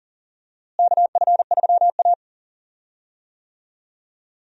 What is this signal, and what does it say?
Morse code, heard as a keyed tone, about 40 wpm, 700 Hz. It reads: KF3A
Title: USA Call Signs - 4 Characters - Repeated 2x - 40wpm